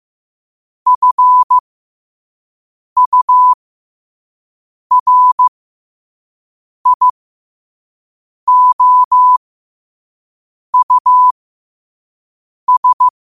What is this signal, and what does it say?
Morse code, heard as a keyed tone, about 15 wpm, spaced as for 5 wpm, 1000 Hz. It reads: FURIOUS
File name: qotdfiles / 5wpm